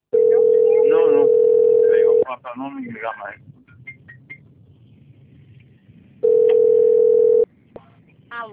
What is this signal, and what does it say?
Ringback tone (North America), 440 + 480 Hz, with speech.
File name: out- - - - - .25049